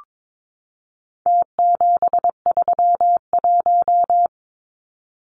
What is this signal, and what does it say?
Morse code, heard as a keyed tone, about 22 wpm, 700 Hz. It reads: T731
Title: Numbers - Contest Style - 4-Digit 22wpm